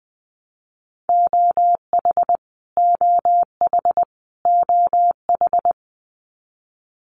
Morse code, heard as a keyed tone, about 20 wpm, 700 Hz. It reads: OH OH OH
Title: US States Repeated 3x 20wpm